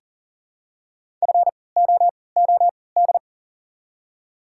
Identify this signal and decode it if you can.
Morse code, heard as a keyed tone, about 40 wpm, spaced as for 23 wpm, 690 Hz. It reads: FOOD
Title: 3x Character Spacing - Top 500 Words - 40wpm